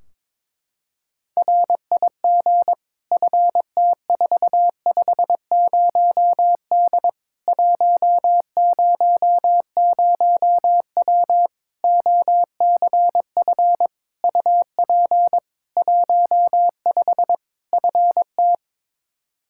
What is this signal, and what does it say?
Morse code, heard as a keyed tone, about 22 wpm, 700 Hz. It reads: RIG FT450D 100W OCF UP 15 FT
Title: Practice QSO - Exchange 2 - Info - 22wpm